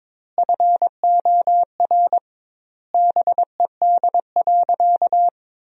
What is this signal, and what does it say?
Morse code, heard as a keyed tone, about 22 wpm, 700 Hz. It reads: FOR BED.
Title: 2x Word-Spacing - Sentences from Top 500 Words - 22wpm